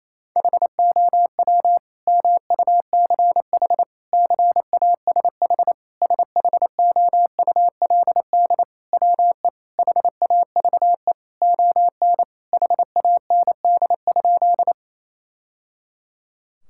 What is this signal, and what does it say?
Morse code, heard as a keyed tone, about 28 wpm, 700 Hz. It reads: HOW MUCH CASH SHOULD WE HAVE ON HAND?
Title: Sentences from Top 1000 Words - Review Part 5 - 28wpm